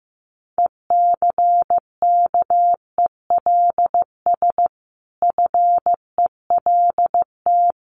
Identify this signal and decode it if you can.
Morse code, heard as a keyed tone, about 15 wpm, 700 Hz. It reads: ECKELS FELT
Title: A Sound Of Thunder 15wpm